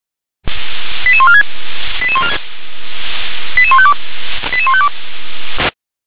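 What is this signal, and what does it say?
Five-tone selective call (ZVEI), heard as quick runs of five tones, tones of about 70 ms. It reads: 88146 88152 88152
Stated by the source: Five-tone ZVEI selective call received on an SDR.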